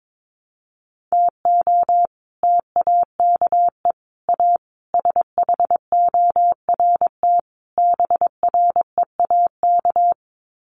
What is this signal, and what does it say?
Morse code, heard as a keyed tone, about 22 wpm, 700 Hz. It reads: TO TAKE A SHORT BREAK
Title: Sets of 5 Words Encore2 22wpm